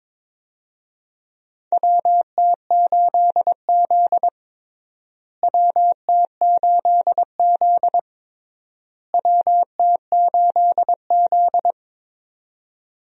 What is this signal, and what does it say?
Morse code, heard as a keyed tone, about 22 wpm, 700 Hz. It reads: WT8Z WT8Z WT8Z
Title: USA Call Signs - 4 Characters - Repeated 3x - 22wpm